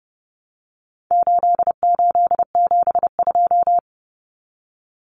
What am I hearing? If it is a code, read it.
Morse code, heard as a keyed tone, about 30 wpm, 700 Hz. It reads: 8872
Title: Numbers - 4-Digit 30wpm